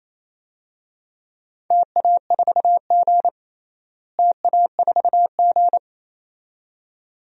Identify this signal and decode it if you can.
Morse code, heard as a keyed tone, about 28 wpm, 700 Hz. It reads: TA4G TA4G
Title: DXCC-100 Call Signs - 4 Characters - Repeated 2x - 28wpm